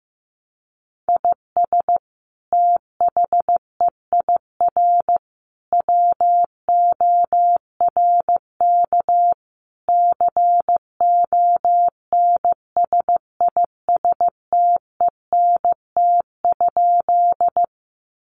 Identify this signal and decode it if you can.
Morse code, heard as a keyed tone, about 15 wpm, 700 Hz. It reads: IS THEIR WORK CONSISTENT?
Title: Sentences from Top 1500 Words 15wpm